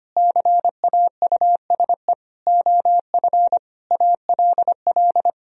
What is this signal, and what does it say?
Morse code, heard as a keyed tone, about 25 wpm, 700 Hz. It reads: CAUSE OF ALL
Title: Sentences from Top 1000 Words - Review Part 1 - 25wpm